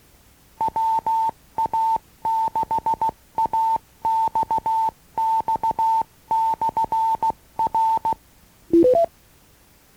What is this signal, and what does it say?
Morse code, heard as a keyed tone, about 16 wpm, 900 Hz. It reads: WA6AXX/R
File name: acc-rc-85-and-rc-96-instruction-tape